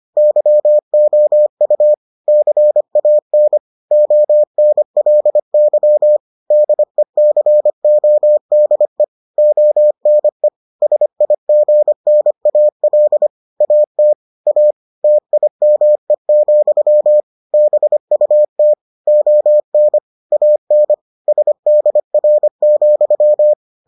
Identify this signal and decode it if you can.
Morse code, heard as a keyed tone, about 25 wpm, 600 Hz. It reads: YOU CAN ONLY DECODE ONE SIGNAL AT A TIME, BUT ON AN SDR,